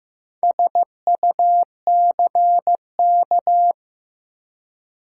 Morse code, heard as a keyed tone, about 15 wpm, 700 Hz. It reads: SUCK
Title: Four Letter Words - 15wpm